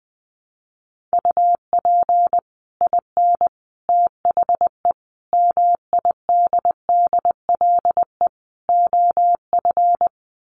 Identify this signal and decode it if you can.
Morse code, heard as a keyed tone, about 20 wpm, 700 Hz. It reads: UP IN THE MIDDLE OF
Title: Sets of 5 Words Encore 20wpm